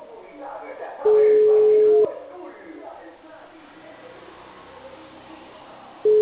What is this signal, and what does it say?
Ringback tone (Europe, 425 Hz), with speech.